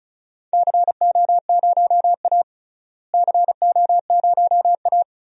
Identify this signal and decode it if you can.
Morse code, heard as a keyed tone, about 36 wpm, 700 Hz. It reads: CO0A CO0A
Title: DXCC-100 Call Signs - 4 Characters - Repeated 2x - 35wpm